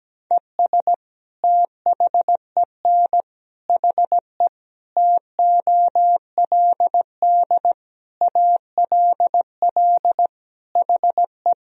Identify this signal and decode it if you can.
Morse code, heard as a keyed tone, about 17 wpm, 700 Hz. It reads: ES THEN HE TOLD ALL HE
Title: Sentences from Top 400 Words and Abbreviations 17wpm